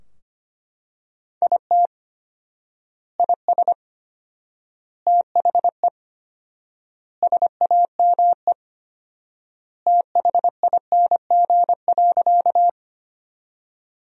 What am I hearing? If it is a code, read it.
Morse code, heard as a keyed tone, about 25 wpm, 700 Hz. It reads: IT IS THE SAME THING.
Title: 4x Word-Spacing - Sentences from Top 200 Words - 25wpm